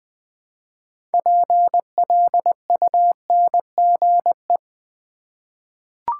Morse code, heard as a keyed tone, about 20 wpm, 700 Hz. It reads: PLUNGE
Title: 6 Letter Words - First 3 Letters Given - 20wpm